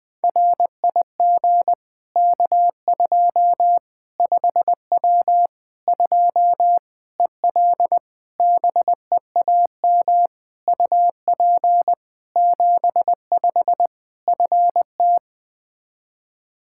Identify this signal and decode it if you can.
Morse code, heard as a keyed tone, about 21 wpm, 700 Hz. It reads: RIG K2 5W 2 EL BEAM UP 75 FT
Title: Practice QSO - Exchange 2 - Info - 20wpm